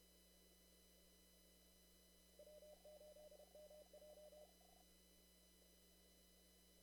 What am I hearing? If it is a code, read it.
Morse code, heard as a keyed tone, about 31 wpm, 600 Hz. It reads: W8MJ
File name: CW / kt5j wpx cw-op-k5pi